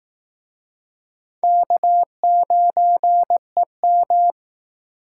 Morse code, heard as a keyed tone, about 19 wpm, 700 Hz. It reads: K9EM
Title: Speed Racing - USA 1x2 Call Signs 15wpm